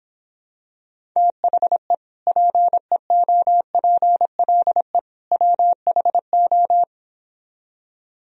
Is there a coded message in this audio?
Yes: Morse code from 1.2 to 6.8 s, heard as a keyed tone, about 26 wpm, 700 Hz. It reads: THE PEOPLE WHO